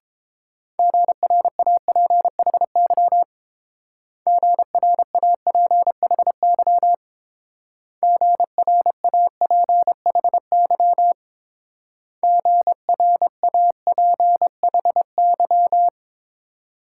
Morse code, heard as a keyed tone, about 27 wpm, 700 Hz. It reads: GRAPHY GRAPHY GRAPHY GRAPHY
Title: Speed-Racing Multi-letter Phonemes - 22wpm